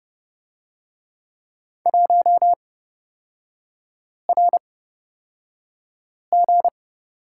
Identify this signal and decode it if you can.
Morse code, heard as a keyed tone, about 31 wpm, spaced as for 5 wpm, 700 Hz. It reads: 1RG
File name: Lesson 38e - Letter q - Three Characters 30wpm